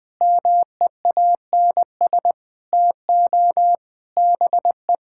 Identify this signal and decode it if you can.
Morse code, heard as a keyed tone, about 20 wpm, 700 Hz. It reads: MEANS TO BE
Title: Sentences with 5-Word Phrases - 20wpm